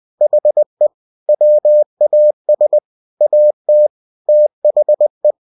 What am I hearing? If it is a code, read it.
Morse code, heard as a keyed tone, about 20 wpm, 600 Hz. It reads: HE WAS AT THE
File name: cw-books / winnie-the-pooh / winnie-20wpm-page-0093